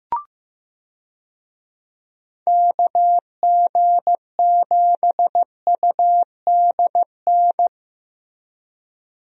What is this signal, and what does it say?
Morse code, heard as a keyed tone, about 15 wpm, 700 Hz. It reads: KG7UDN
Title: USA Call Signs - 6 Characters - Repeated 3x - 15wpm